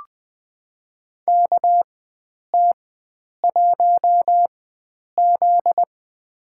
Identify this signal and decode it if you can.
Morse code, heard as a keyed tone, about 21 wpm, spaced as for 9 wpm, 700 Hz. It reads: KT1Z
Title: USA Call Signs - 4 Characters - 4x Character Spacing - 20wpm